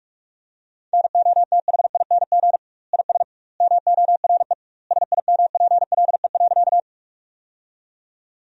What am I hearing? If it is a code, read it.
Morse code, heard as a keyed tone, about 46 wpm, 700 Hz. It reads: NOTHING IS MORE SIMPLE.
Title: 2x Word-Spacing - Sentences from Top 500 Words - 45wpm